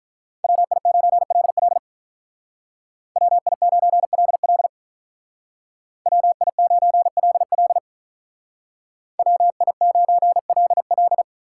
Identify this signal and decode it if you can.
Morse code, heard as a keyed tone, about 43 wpm, 700 Hz. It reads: WI9LL WI9LL WI9LL WI9LL
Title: Speed Racing - USA Call Signs 35wpm